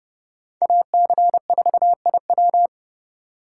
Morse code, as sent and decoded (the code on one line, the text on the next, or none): .- -.-. ....- .. .--
AC4IW